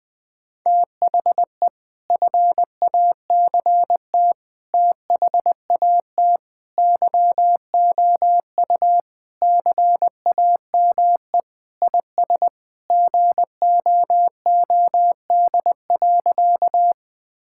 Morse code, heard as a keyed tone, about 20 wpm, 700 Hz. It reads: THE FACT THAT YOU CAME IS GOOD.
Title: Sentences with 3-Word Phrases - Spoken Beginning - 20wpm